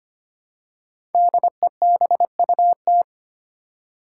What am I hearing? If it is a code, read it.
Morse code, heard as a keyed tone, about 25 wpm, 700 Hz. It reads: DEBUT
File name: 5 Letter Words - First 3 Letters Given - 25wpm